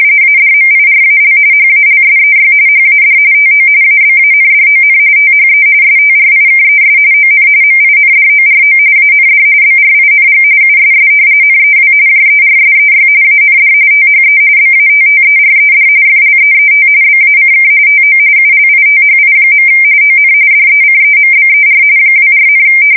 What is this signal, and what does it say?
RTTY, about 45.45 baud. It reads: ADMIRAL HALSEY HAD TO ABANDON THE STRIKES ON MINDORO AND BRING USS NEW JERSEY AND THE BATTERED FLEET BACK TO ULITHI. ON DECEMBER 24,
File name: fsk170 / battleshipnj